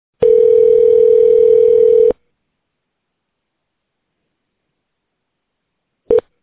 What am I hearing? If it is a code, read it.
Ringback tone (North America), 440 + 480 Hz.